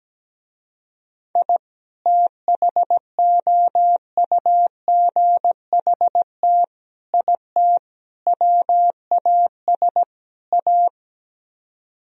Morse code, heard as a keyed tone, about 17 wpm, 700 Hz. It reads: I THOUGHT IT WAS A
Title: Speed-Racing - Sets of 5 Words 17wpm